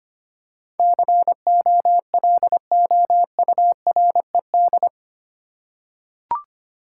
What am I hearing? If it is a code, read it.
Morse code, heard as a keyed tone, about 25 wpm, 700 Hz. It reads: COLOURED
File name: Eight Letter Words - 25wpm